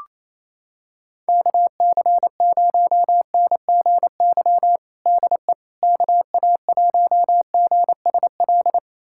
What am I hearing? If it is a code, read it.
Morse code, heard as a keyed tone, about 27 wpm, 700 Hz. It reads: KC0NGY DE KA1GSL
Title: USA Callsign DE Callsign - 6 Characters - 28wpm